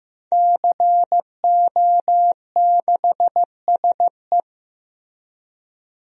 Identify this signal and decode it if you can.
Morse code, heard as a keyed tone, about 15 wpm, 700 Hz. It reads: CO6SE
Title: DXCC-100 Call Signs - 5 Characters - Repeated 2x - 15wpm